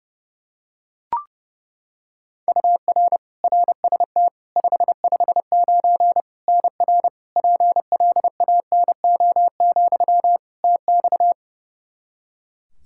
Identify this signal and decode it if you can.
Morse code, heard as a keyed tone, about 30 wpm, 700 Hz. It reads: UR RST 559 NR PLANO, TX
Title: Practice QSO - Exchange 1 - Beginning - 30wpm